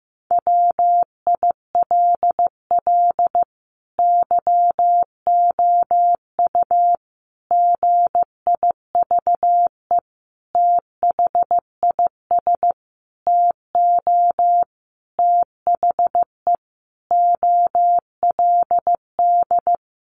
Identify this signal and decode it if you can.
Morse code, heard as a keyed tone, about 15 wpm, 700 Hz. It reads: WILL YOU GIVE THIS TO THE OLD
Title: Sentences from Top 200 Words 15wpm